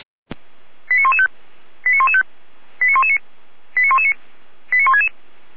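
Five-tone selective call (ZVEI), heard as quick runs of five tones, tones of about 70 ms. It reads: 89115 89115 89119 89119 89166
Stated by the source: ZVEI five-tone selective call 89067 received on SDR by fire brigade Feuerwehr Orlamünde.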